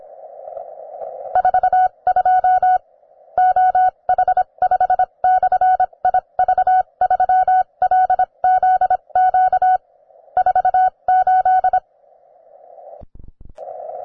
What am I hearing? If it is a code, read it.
Morse code, heard as a keyed tone, about 25 wpm, spaced as for 22 wpm, 740 Hz. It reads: 42 OH5/IV3LZQ 48